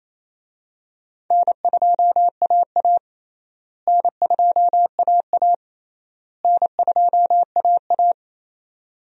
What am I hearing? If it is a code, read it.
Morse code, heard as a keyed tone, about 28 wpm, 700 Hz. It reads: N2AA N2AA N2AA
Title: USA Call Signs - 4 Characters - Repeated 3x - 28wpm